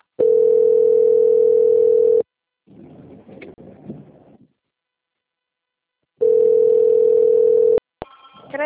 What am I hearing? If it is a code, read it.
Ringback tone (North America), 440 + 480 Hz.